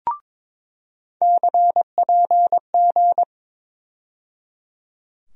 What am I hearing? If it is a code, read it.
Morse code, heard as a keyed tone, about 22 wpm, 700 Hz. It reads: CPG
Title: USA Call Sign Postfixes - 3 Letters 22wpm